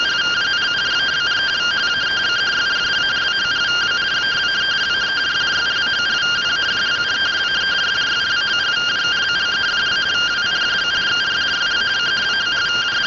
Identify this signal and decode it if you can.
Two-tone teleprinter signal, about 170 Hz shift.